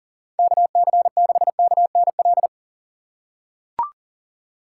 Morse code, heard as a keyed tone, about 38 wpm, 690 Hz. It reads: KC6KNL
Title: USA Call Signs - 6 Characters - 2x Character Spacing - 40wpm